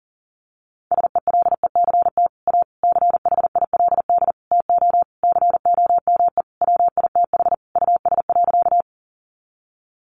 Morse code, heard as a keyed tone, about 40 wpm, 700 Hz. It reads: SELECT A CHILD TO COME WITH US.
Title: Sentences from Top 1300 Words 40wpm